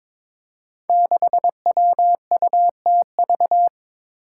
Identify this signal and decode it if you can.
Morse code, heard as a keyed tone, about 22 wpm, 700 Hz. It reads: 6WUTV
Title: Five Challenging Letter-Number Combinations 22wpm